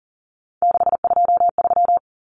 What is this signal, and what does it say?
Morse code, heard as a keyed tone, about 37 wpm, 700 Hz. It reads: 623
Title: Common Cut Numbers - 3-Digit 40wpm